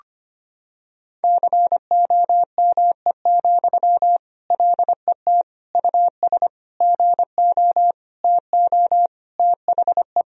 Morse code, heard as a keyed tone, about 25 wpm, 700 Hz. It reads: COME, LET US GO TO THE
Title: Sentences from Top 1000 Words - Review Part 4 - 25wpm